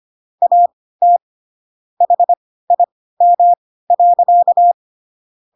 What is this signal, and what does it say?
Morse code, heard as a keyed tone, about 25 wpm, spaced as for 16 wpm, 700 Hz. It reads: AT HIM.
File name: Pinocchio / Pinocchio ch3 10 25 ik1hge